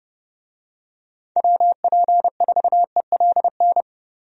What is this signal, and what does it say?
Morse code, heard as a keyed tone, about 30 wpm, 700 Hz. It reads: WP4ELN
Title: USA Call Signs - 6 Characters - Repeated 3x - 30wpm